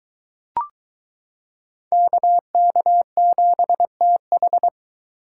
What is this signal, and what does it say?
Morse code, heard as a keyed tone, about 22 wpm, 700 Hz. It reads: KK7TH